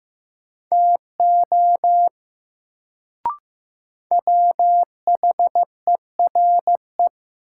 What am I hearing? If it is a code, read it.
Morse code, heard as a keyed tone, about 15 wpm, 700 Hz. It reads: TO WHERE
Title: Sets of 2 Words - New Word Drill - 15wpm